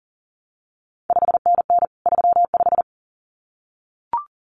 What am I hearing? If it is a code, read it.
Morse code, heard as a keyed tone, about 42 wpm, 700 Hz. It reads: 5NN 35
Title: CQ WW DX Contest Exchange Encore - 40wpm